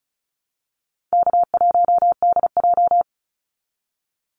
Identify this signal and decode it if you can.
Morse code, heard as a keyed tone, about 34 wpm, 700 Hz. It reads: K1DJ
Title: Easy Call Signs Repeated 3x - 35wpm